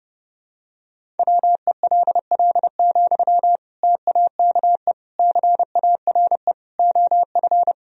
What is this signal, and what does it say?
Morse code, heard as a keyed tone, about 30 wpm, 700 Hz. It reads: WELL, TAKE CARE OF
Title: Sentences from Top 1000 Words - Review Part 1 - 30wpm